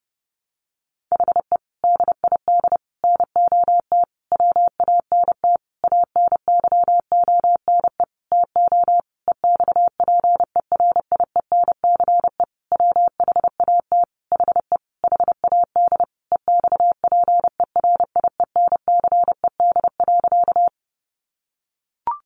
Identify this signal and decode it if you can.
Morse code, heard as a keyed tone, about 30 wpm, 700 Hz. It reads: HE DID NOT WANT ANYONE TO EXPERIENCE WHAT HE HAD EXPERIENCED.
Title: Sentences from Top 1100 Words 30wpm